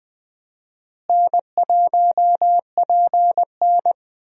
Morse code, heard as a keyed tone, about 21 wpm, 700 Hz. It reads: N1PN
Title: USA Call Signs - 4 Characters - Repeated 2x - 20wpm